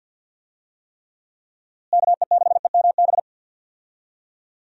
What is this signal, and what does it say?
Morse code, heard as a keyed tone, about 52 wpm, 700 Hz. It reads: KE6EMB
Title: USA Call Signs - 6 Characters - Repeated 2x - 50wpm